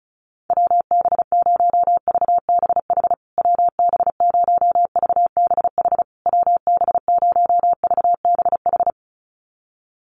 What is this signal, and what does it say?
Morse code, heard as a keyed tone, about 34 wpm, 700 Hz. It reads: WB0VBH WB0VBH WB0VBH
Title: USA Call Signs Repeated 3x 35wpm